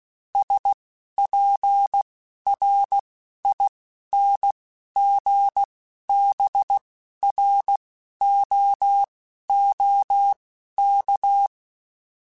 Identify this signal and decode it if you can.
Morse code, heard as a keyed tone, about 16 wpm, spaced as for 11 wpm, 800 Hz. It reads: SPRINGBROOK